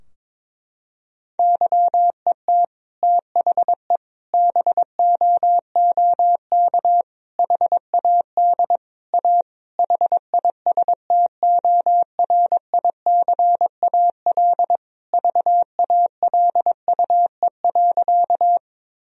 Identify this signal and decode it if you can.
Morse code, heard as a keyed tone, about 21 wpm, 700 Hz. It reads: YET THE BOOK HAD A HISTORICAL VALUE.
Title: Sentences from Top 1000 Words - Review Part 4 - 22wpm